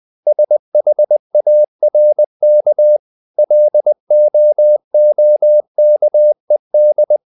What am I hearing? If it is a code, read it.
Morse code, heard as a keyed tone, about 20 wpm, 600 Hz. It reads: SHARK LOOKED
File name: the old man and the sea part 2 20wpm